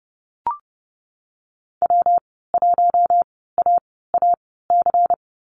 Morse code, heard as a keyed tone, about 32 wpm, spaced as for 17 wpm, 700 Hz. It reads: W1AAC
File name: Lesson 19g - Letter w - Call Signs 30wpm